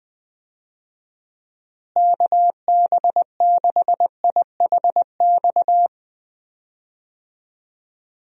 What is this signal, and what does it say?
Morse code, heard as a keyed tone, about 20 wpm, 700 Hz. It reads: KB6IHX
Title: USA Call Signs - 6 Characters - Repeated 3x - 20wpm